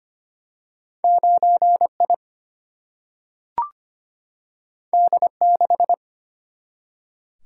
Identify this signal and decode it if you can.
Morse code, heard as a keyed tone, about 25 wpm, 700 Hz. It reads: 9 I D 6
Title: Single-Letter Single-Number 25wpm